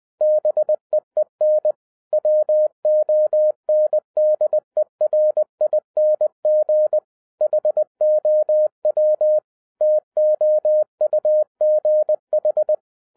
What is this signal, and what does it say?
Morse code, heard as a keyed tone, about 20 wpm, 610 Hz. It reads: BEEN WONDERING HOW TOUGH
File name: 2012-01-mp3 / n16-THROUGH EIGHT MONTHS-20wpm